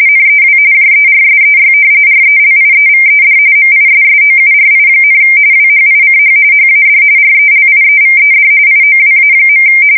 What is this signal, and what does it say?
RTTY, about 45.45 baud. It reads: NEW JERSEY RELIEVED MISSOURI, WHICH HAD RELIEVED IOWA.
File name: fsk170 / battleshipnj